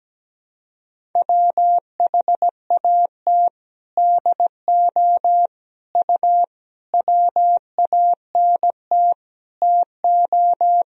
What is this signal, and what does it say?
Morse code, heard as a keyed tone, about 17 wpm, 700 Hz. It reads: WHAT DO U WANT TO
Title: Sentences from Top 200 Words and Abbreviations 17wpm